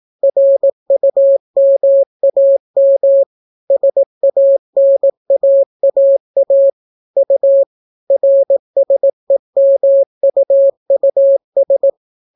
Morse code, heard as a keyed tone, about 18 wpm, 550 Hz. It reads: RUMAM SANAAA U RSEMUUS